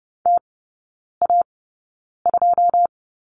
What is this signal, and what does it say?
Morse code, heard as a keyed tone, about 30 wpm, spaced as for 9 wpm, 700 Hz. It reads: TA2
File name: 081 Lesson 14e - Number 2 - Three Characters 30wpm